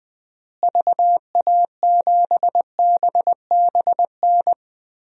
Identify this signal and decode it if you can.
Morse code, heard as a keyed tone, about 20 wpm, 700 Hz. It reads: VA7BBN
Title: DXCC-100 Call Signs - 6 Characters - Repeated 3x - 20wpm